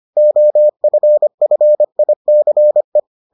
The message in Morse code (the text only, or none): OFFICE